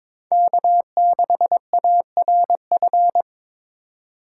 Morse code, heard as a keyed tone, about 21 wpm, 700 Hz. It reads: K6ARF
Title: USA Call Signs - 3x Character Spacing - 22wpm